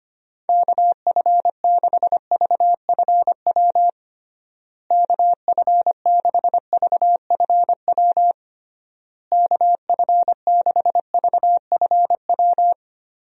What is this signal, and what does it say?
Morse code, heard as a keyed tone, about 25 wpm, 700 Hz. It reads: KF6VFW KF6VFW KF6VFW
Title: USA Call Signs - 6 Characters - Repeated 3x - 25wpm